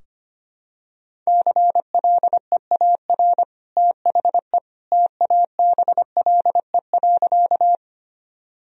Morse code, heard as a keyed tone, about 25 wpm, 700 Hz. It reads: CLEAR THE TABLE.
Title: Sentences from Top 1000 Words - Review Part 2 - 25wpm